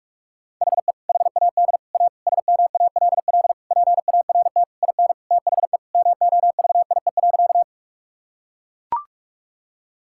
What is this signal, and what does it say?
Morse code, heard as a keyed tone, about 41 wpm, 700 Hz. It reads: HE HAD A SMALL PART IN THE MOVIE.
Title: Sentences from Top 1000 Words - Review Part 5 - 45wpm